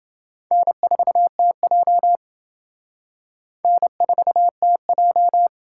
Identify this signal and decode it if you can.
Morse code, heard as a keyed tone, about 28 wpm, 700 Hz. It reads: N4TJ N4TJ